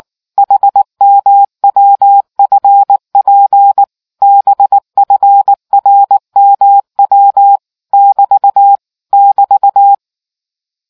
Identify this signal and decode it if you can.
Morse code, heard as a keyed tone, about 19 wpm, 800 Hz. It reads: HMWFP BFRMW <BT> <BT>